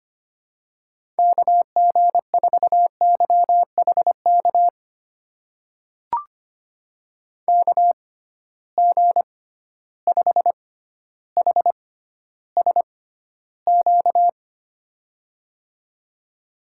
Morse code, heard as a keyed tone, about 25 wpm, spaced as for 13 wpm, 700 Hz. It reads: KG4YHK KG5HSQ
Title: USA Call Signs - 6 Characters - 6x Character Spacing - 25wpm